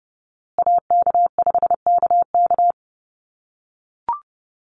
Morse code, heard as a keyed tone, about 30 wpm, 700 Hz. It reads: AK5KK
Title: USA Call Signs Repeated 2x 30wpm